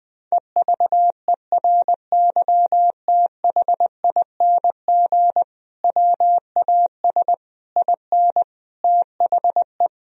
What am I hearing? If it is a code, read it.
Morse code, heard as a keyed tone, about 20 wpm, 700 Hz. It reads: EVERYTHING WAS IN THE
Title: Sentences from Top 1000 Words - Review Part 2 - 20wpm